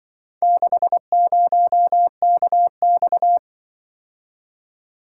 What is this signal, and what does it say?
Morse code, heard as a keyed tone, about 24 wpm, 700 Hz. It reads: 60KX
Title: Speed-Racing - Four Letter-Number Combinations - 20wpm